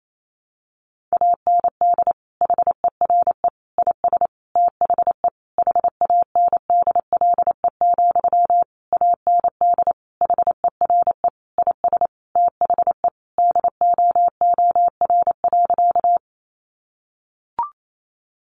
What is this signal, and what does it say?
Morse code, heard as a keyed tone, about 28 wpm, 700 Hz. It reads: AND HERE IS THE HANDLE, AND HERE IS THE DOOR.
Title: Sentences from Top 1100 Words 28wpm